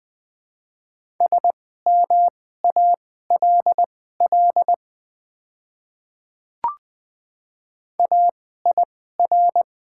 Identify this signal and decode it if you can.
Morse code, heard as a keyed tone, about 20 wpm, spaced as for 14 wpm, 700 Hz. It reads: SMALL AIR
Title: 2x Character Spacing - Top 200 Words - 20wpm